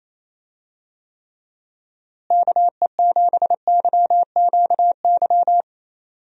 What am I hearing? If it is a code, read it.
Morse code, heard as a keyed tone, about 28 wpm, 700 Hz. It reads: KE7YQY
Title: USA Call Signs - 6 Characters - Repeated 3x - 28wpm